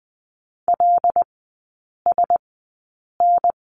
Morse code, heard as a keyed tone, about 20 wpm, spaced as for 8 wpm, 700 Hz. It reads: LSN
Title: 127 Lesson 21e - Number 6 - Three Characters 20wpm